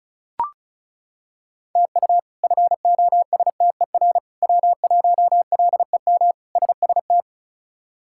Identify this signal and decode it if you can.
Morse code, heard as a keyed tone, about 36 wpm, 700 Hz. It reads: TU FOSTER W1LEM SST
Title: K1USN SST QSO Practice - 35wpm